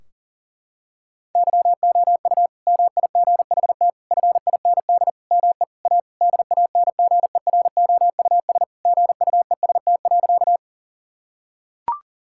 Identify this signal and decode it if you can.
Morse code, heard as a keyed tone, about 40 wpm, 700 Hz. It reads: YOU MIGHT FIND ME A DANGEROUS GUEST.
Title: Sentences from Top 1600 Words 40wpm